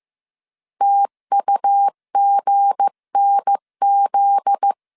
Morse code, heard as a keyed tone, about 15 wpm, 800 Hz. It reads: TUGNZ
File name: CW / DET / 1 9 1500C S B 14wpm 3302